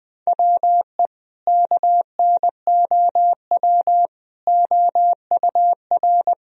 Morse code, heard as a keyed tone, about 20 wpm, 700 Hz. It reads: WE KNOW OUR
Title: Sentences from Top 1000 Words - Review Part 4 - 20wpm